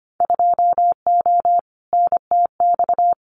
Morse code, heard as a keyed tone, about 25 wpm, 690 Hz. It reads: 2O NTX
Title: Winter Field Day Exchanges 25wpm